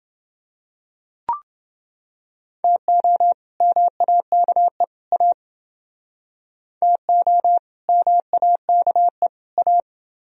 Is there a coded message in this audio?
Yes: Morse code from 2.6 to 9.8 s, heard as a keyed tone, about 30 wpm, 700 Hz. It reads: TO MAKE A TO MAKE A